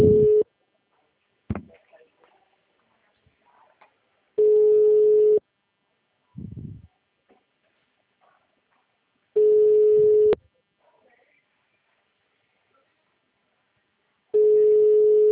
Ringback tone (Europe, 425 Hz).
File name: out- -2006- - - .21549